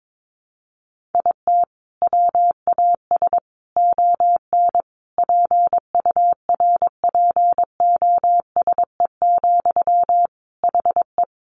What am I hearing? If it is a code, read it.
Morse code, heard as a keyed tone, about 22 wpm, 700 Hz. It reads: IT WAS ON PURPOSE, HE
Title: Sentences from Top 800 Words 22wpm